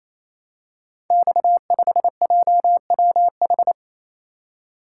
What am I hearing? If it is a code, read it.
Morse code, heard as a keyed tone, about 28 wpm, 700 Hz. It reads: X5JWH
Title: Code Groups - 5 Letter:Numbers - 28wpm